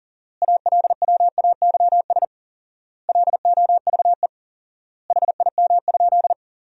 Morse code, heard as a keyed tone, about 37 wpm, 700 Hz. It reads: ALWAYS LOVE HIM?
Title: 4x Word-Spacing - Sentences from Top 400 Words - 40wpm